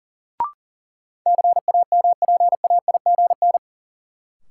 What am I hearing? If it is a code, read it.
Morse code, heard as a keyed tone, about 40 wpm, 700 Hz. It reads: CAMPAIGN
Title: Sentences from Top 1200 Words - New Word Drill Alt - 40wpm